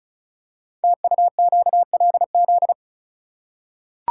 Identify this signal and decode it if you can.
Morse code, heard as a keyed tone, about 36 wpm, 700 Hz. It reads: TUQLZ
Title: Code Groups - 5 Letters - 35wpm